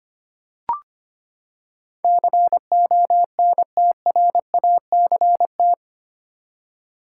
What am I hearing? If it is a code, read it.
Morse code, heard as a keyed tone, about 25 wpm, 700 Hz. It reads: CONTRACT
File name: Eight Letter Words - 25wpm